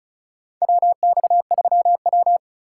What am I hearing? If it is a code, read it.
Morse code, heard as a keyed tone, about 34 wpm, 700 Hz. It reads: WX3W
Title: USA Call Signs - 4 Characters - 4x Character Spacing - 35wpm